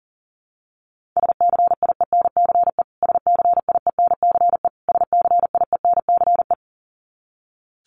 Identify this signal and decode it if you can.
Morse code, heard as a keyed tone, about 37 wpm, 700 Hz. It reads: SCIENCE SCIENCE SCIENCE
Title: Top 500 Words Repeated 40wpm